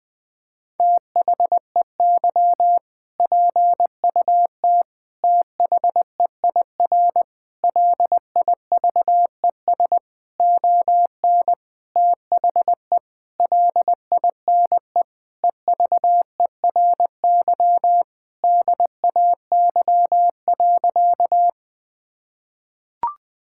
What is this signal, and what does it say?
Morse code, heard as a keyed tone, about 20 wpm, 700 Hz. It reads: THEY PUT THEIR LIVES ON THE LINE EVERY DAY.
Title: Sentences with 8-Word Phrases - 20wpm